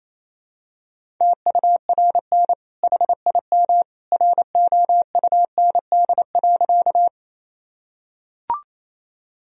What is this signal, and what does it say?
Morse code, heard as a keyed tone, about 28 wpm, 700 Hz. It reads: TURN HIM ROUND.
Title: Sentences from Top 1000 Words - Review Part 1 - 28wpm